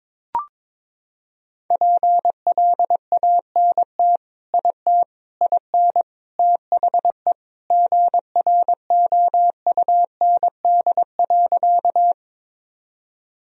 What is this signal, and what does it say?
Morse code, heard as a keyed tone, about 22 wpm, 700 Hz. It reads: PLANT IT IN THE GROUND.
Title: Sentences from Top 1000 Words - Review Part 1 - 22wpm